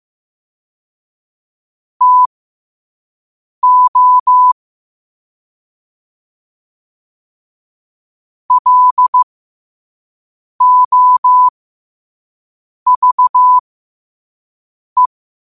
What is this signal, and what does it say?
Morse code, heard as a keyed tone, about 15 wpm, spaced as for 5 wpm, 1000 Hz. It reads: TO LOVE
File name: qotdfiles / 5wpm